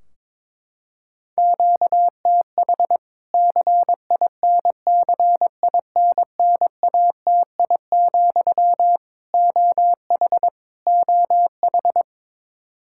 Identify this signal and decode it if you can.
Morse code, heard as a keyed tone, about 22 wpm, 700 Hz. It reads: QTH CINCINNATI, OH OH
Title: K1USN SST QSO Practice 22wpm